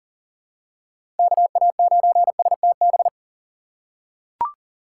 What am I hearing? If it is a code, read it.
Morse code, heard as a keyed tone, about 38 wpm, 700 Hz. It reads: KA9STB
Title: USA Call Signs - 6 Characters - 4x Character Spacing - 40wpm